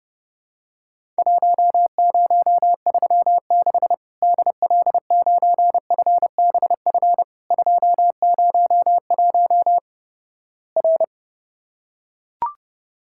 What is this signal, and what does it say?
Morse code, heard as a keyed tone, about 30 wpm, 700 Hz. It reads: 1036 DL9FBF 201 R
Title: Worked All Europe DX Contest - QTC Exchange - 30wpm